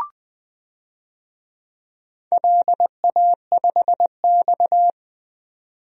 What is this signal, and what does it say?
Morse code, heard as a keyed tone, about 20 wpm, 700 Hz. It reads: LA5X
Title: DXCC-100 Call Signs - 4 Characters - 20wpm